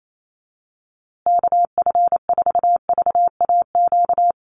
Morse code, heard as a keyed tone, about 28 wpm, 700 Hz. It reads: KF4VAQ
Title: USA Call Signs Repeated 3x 28wpm